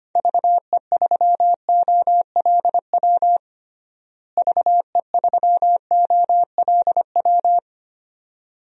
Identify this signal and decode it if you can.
Morse code, heard as a keyed tone, about 25 wpm, 700 Hz. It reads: VE3OLW VE3OLW
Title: DXCC-100 Call Signs - 6 Characters - Repeated 2x - 25wpm